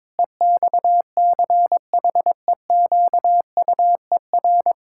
Morse code, heard as a keyed tone, about 22 wpm, 700 Hz. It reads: EXCHEQUER
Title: Pangram Challenge v2 - 22wpm